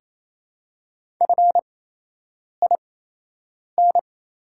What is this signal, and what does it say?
Morse code, heard as a keyed tone, about 27 wpm, spaced as for 8 wpm, 700 Hz. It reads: FIN
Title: 8x Character Spacing - Top 500 Words - 28wpm